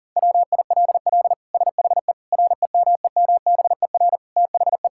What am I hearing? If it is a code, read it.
Morse code, heard as a keyed tone, about 40 wpm, 700 Hz. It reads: WILL SHE REMEMBER THE
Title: Sentences from Top 1000 Words - Review Part 6 - 40wpm